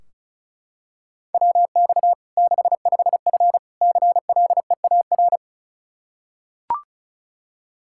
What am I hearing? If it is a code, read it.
Morse code, heard as a keyed tone, about 34 wpm, 700 Hz. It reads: WX 65F CLEAR
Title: Practice QSO - Exchange 2 - Info - 35wpm